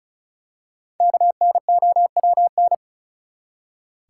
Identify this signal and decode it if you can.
Morse code, heard as a keyed tone, about 36 wpm, 700 Hz. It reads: KNOWN
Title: Five Letter Words - 35wpm